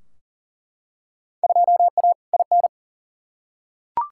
Morse code, heard as a keyed tone, about 40 wpm, 700 Hz. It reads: 2A IN
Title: Field Day Exchanges - 40wpm